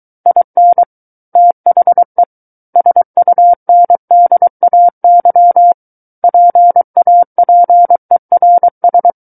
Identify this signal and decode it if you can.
Morse code, heard as a keyed tone, about 23 wpm, 700 Hz. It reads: IN THE SUNDAY PAPERS
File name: Book One / 14-Chapter 14 - 23 WPM